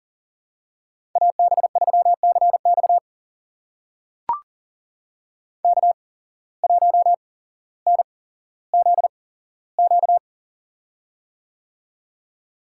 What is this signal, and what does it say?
Morse code, heard as a keyed tone, about 40 wpm, spaced as for 17 wpm, 700 Hz. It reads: AB3CX K1NZQ
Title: USA Call Signs - 5 Characters - 8x Character Spacing - 40wpm